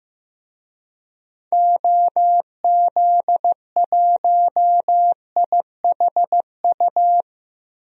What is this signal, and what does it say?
Morse code, heard as a keyed tone, about 15 wpm, 700 Hz. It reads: OZ1IHU
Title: DXCC-100 Call Signs - 6 Characters - Repeated 2x - 15wpm